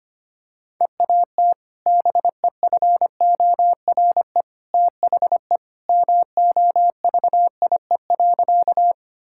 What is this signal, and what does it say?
Morse code, heard as a keyed tone, about 25 wpm, 700 Hz. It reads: EAT BEFORE THE MOVIE.
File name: practice / 25wpm / Sentences with 2-Word Phrases - 25wpm